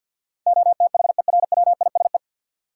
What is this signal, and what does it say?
Morse code, heard as a keyed tone, about 46 wpm, 700 Hz. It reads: OTHERWISE
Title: Sentences from Top 1200 Words - New Word Drill Alt - 50wpm